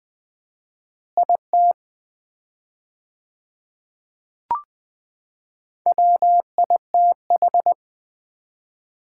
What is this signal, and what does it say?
Morse code, heard as a keyed tone, about 20 wpm, 690 Hz. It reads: IT WITH
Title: 4x Word-Spacing - Sets of 2 Words - 20wpm